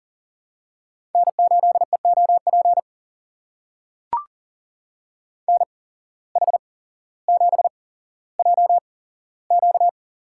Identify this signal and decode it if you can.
Morse code, heard as a keyed tone, about 40 wpm, spaced as for 17 wpm, 700 Hz. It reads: N8EOP NH7JQ
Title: USA Call Signs - 5 Characters - 8x Character Spacing - 40wpm